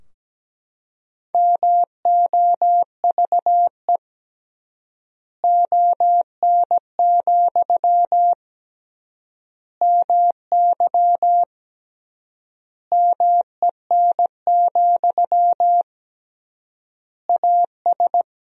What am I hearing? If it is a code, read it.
Morse code, heard as a keyed tone, about 17 wpm, 700 Hz. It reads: MOVE ON, MY MEN, AS
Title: 3x Word-Spacing - Sentences from Top 500 Words - 17wpm